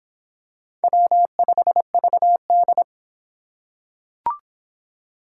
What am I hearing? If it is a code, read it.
Morse code, heard as a keyed tone, about 26 wpm, 700 Hz. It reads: W5VD